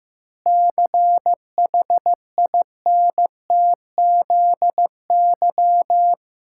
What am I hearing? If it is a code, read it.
Morse code, heard as a keyed tone, about 14 wpm, 700 Hz. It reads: CHINTZY
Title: Pangram Challenge v2 - 15wpm